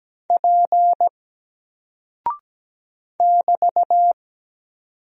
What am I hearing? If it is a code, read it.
Morse code, heard as a keyed tone, about 17 wpm, 700 Hz. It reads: P <BT>
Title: Single Character (Letter:Number:Punctuation:Prosign) Alt 17wpm